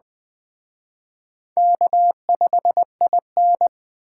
Morse code, heard as a keyed tone, about 20 wpm, 700 Hz. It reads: K5IN